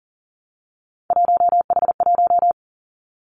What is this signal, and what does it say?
Morse code, heard as a keyed tone, about 42 wpm, 700 Hz. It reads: 1H1
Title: Three Challenging Letter-Number Combinations 40wpm